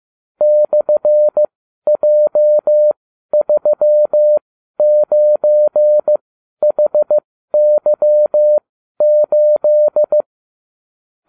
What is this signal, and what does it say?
Morse code, heard as a keyed tone, about 15 wpm, spaced as for 12 wpm, 600 Hz. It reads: /J39HY8
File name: cw / ubacw29 2